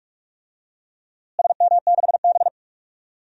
Morse code, heard as a keyed tone, about 47 wpm, 700 Hz. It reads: SM6B
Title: DXCC-100 Call Signs - 4 Characters - Repeated 3x - 45wpm